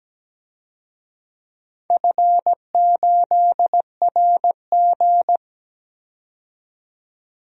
Morse code, heard as a keyed tone, about 17 wpm, 700 Hz. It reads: F8RG
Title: DXCC-100 Call Signs - 4 Characters - 17wpm